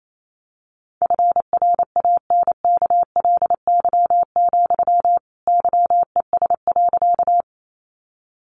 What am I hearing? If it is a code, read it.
Morse code, heard as a keyed tone, about 28 wpm, 700 Hz. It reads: FRANKLY, YES.
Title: A Sound Of Thunder 28wpm